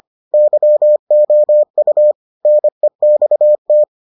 Morse code, heard as a keyed tone, about 25 wpm, 600 Hz. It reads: YOU NEXT